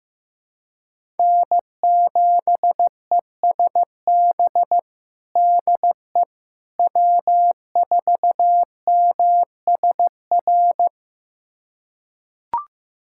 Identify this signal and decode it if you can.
Morse code, heard as a keyed tone, about 15 wpm, 700 Hz. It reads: N7ESB DE W4MSR
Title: USA Callsign DE Callsign - 5 Characters - 15wpm